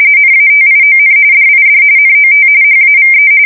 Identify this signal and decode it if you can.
RTTY, about 45.45 baud. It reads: NATION DID NOT HAVE
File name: fsk170 / usnavy1